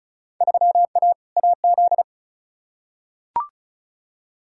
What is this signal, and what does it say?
Morse code, heard as a keyed tone, about 34 wpm, 700 Hz. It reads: 3A AZ
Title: Field Day Exchanges Encore - 35wpm